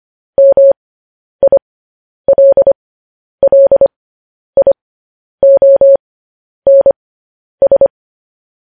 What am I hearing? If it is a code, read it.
Morse code, heard as a keyed tone, about 25 wpm, spaced as for 10 wpm, 550 Hz. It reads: MILLIONS
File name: podcast / News headlines-1x-25x10